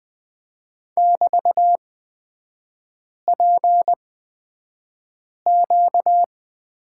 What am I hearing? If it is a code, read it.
Morse code, heard as a keyed tone, about 20 wpm, spaced as for 5 wpm, 700 Hz. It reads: <BT>PQ
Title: 236 Lesson 38e - Letter q - Three Characters 20wpm